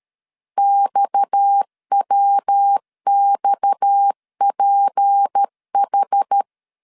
Morse code, heard as a keyed tone, about 13 wpm, 800 Hz. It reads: XWXPH
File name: LEZIONI / 6 PKX0 / 6 PKX 12wpm 2024